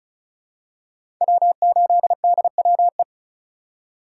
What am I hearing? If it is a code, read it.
Morse code, heard as a keyed tone, about 36 wpm, 700 Hz. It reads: W8DWE
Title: USA Call Signs - 5 Characters - 3x Character Spacing - 35wpm